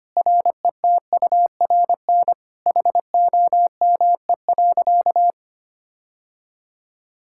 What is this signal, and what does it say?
Morse code, heard as a keyed tone, about 25 wpm, 700 Hz. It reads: RETURN HOME.
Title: Sentences from Top 1600 Words 25wpm